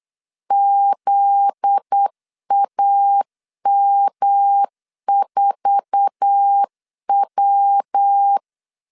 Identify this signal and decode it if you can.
Morse code, heard as a keyed tone, about 8 wpm, 800 Hz. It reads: ZAM4W
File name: CW / DET / 1 9 1500C S C 8wpm 5653